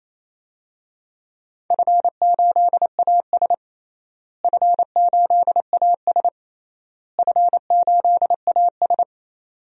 Morse code, heard as a keyed tone, about 28 wpm, 700 Hz. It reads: F8AS F8AS F8AS
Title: DXCC-100 Call Signs - 4 Characters - Repeated 3x - 28wpm